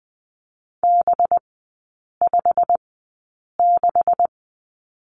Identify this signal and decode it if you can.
Morse code, heard as a keyed tone, about 20 wpm, spaced as for 8 wpm, 700 Hz. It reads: B56
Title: 191 Lesson 31e - Letter v - Three Characters 20wpm